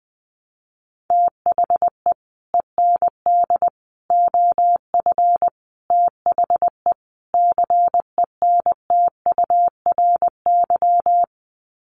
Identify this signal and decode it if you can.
Morse code, heard as a keyed tone, about 20 wpm, 700 Hz. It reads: THE END OF THE CENTURY
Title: Sets of 5 Words Encore2 20wpm